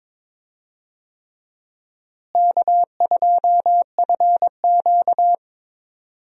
Morse code, heard as a keyed tone, about 22 wpm, 700 Hz. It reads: K2FQ
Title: USA Call Signs - 4 Characters - Repeated 3x - 22wpm